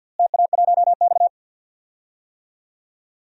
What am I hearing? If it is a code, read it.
Morse code, heard as a keyed tone, about 47 wpm, 700 Hz. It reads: TA1X
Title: DXCC-100 Call Signs - 4 Characters - Repeated 3x - 50wpm